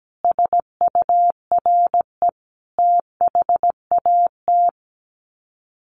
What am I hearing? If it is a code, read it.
Morse code, heard as a keyed tone, about 17 wpm, 700 Hz. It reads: SURE THAT
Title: Sets of 4 Words 17wpm